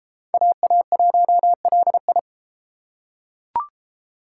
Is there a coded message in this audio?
Yes: Morse code from 0.3 to 2.2 s, heard as a keyed tone, about 32 wpm, 700 Hz. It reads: AA1LI